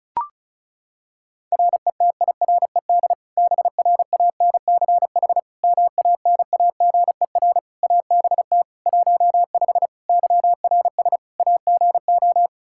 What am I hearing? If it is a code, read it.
Morse code, heard as a keyed tone, about 36 wpm, 700 Hz. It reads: RETIRED BRANCH MANAGER ABT 15 YRS AGO
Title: Practice QSO - Exchange 3 - More Info - 35wpm